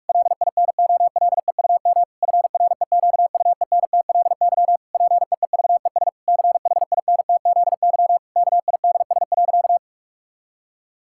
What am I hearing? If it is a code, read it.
Morse code, heard as a keyed tone, about 43 wpm, 700 Hz. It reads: LINOLEUM FREQUENTLY PEEVES CHINTZY KIDS.
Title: Pangram Challenge v2 - 45wpm